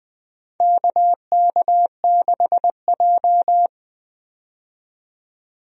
Morse code, heard as a keyed tone, about 20 wpm, 700 Hz. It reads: KK6J
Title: USA Call Signs - 4 Characters - 20wpm